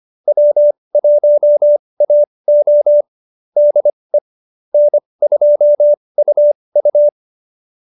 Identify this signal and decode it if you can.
Morse code, heard as a keyed tone, about 25 wpm, spaced as for 20 wpm, 570 Hz. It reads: W1AO DE N2UU
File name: QSO313 20